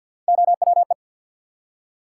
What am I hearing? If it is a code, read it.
Morse code, heard as a keyed tone, about 47 wpm, 700 Hz. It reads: OWE
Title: Three Letter Words - 50wpm